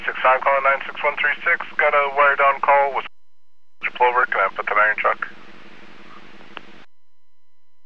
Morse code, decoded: none

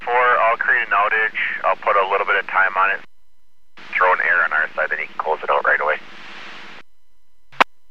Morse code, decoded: none

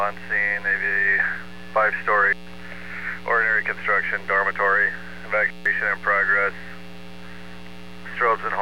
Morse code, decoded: none

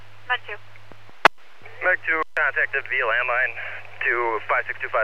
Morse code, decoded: none

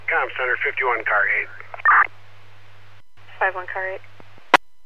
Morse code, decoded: none

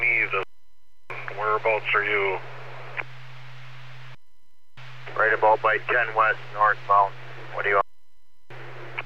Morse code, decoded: none

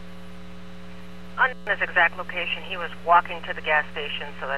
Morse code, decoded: none

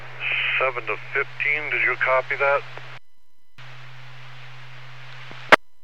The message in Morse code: none

